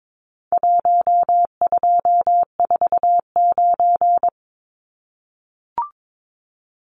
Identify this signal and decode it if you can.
Morse code, heard as a keyed tone, about 22 wpm, 700 Hz. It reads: 1249
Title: Numbers - 4-Digit 22wpm